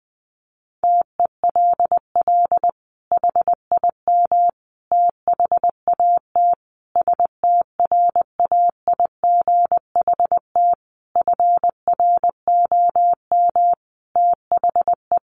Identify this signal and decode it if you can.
Morse code, heard as a keyed tone, about 20 wpm, 700 Hz. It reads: TELL HIM THAT STRAIGHT FROM THE
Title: Sentences from Top 1500 Words 20wpm